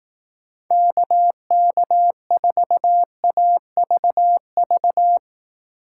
Morse code, heard as a keyed tone, about 18 wpm, 700 Hz. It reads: KK4AVV
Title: Speed Racing - USA 2x3 Call Signs 15wpm